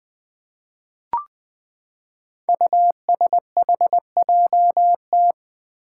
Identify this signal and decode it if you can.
Morse code, heard as a keyed tone, about 20 wpm, 700 Hz. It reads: USHJT
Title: Five Challenging Letter-Number Combinations 20wpm